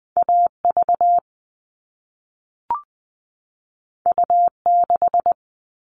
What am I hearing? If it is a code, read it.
Morse code, heard as a keyed tone, about 20 wpm, 700 Hz. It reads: AV U6
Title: All Cut Numbers - 2-Digit 20wpm